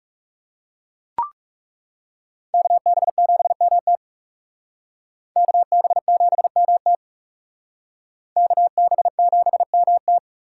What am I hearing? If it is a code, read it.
Morse code, heard as a keyed tone, about 36 wpm, 700 Hz. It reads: KB7MT KB7MT KB7MT
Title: Speed Racing - USA Call Signs 30wpm